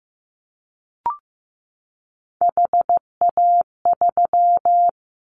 Morse code, heard as a keyed tone, about 15 wpm, 700 Hz. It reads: HA3
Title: Callsign Prefixes for Top 50 DXCC - 15wpm